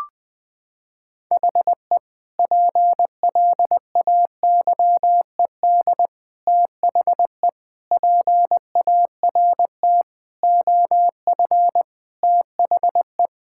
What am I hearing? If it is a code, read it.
Morse code, heard as a keyed tone, about 20 wpm, 700 Hz. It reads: HE PLAYED THE PART OF THE
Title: Sentences with 4-Word Phrases - 20wpm